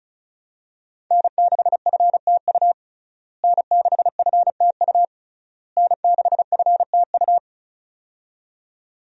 Morse code, heard as a keyed tone, about 36 wpm, 700 Hz. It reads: N6FTU N6FTU N6FTU
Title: USA Call Signs - 5 Characters - Repeated 3x - 35wpm